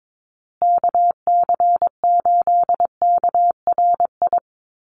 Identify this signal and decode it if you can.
Morse code, heard as a keyed tone, about 22 wpm, 700 Hz. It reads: KC8KRI
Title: USA Call Signs Repeated 3x 22wpm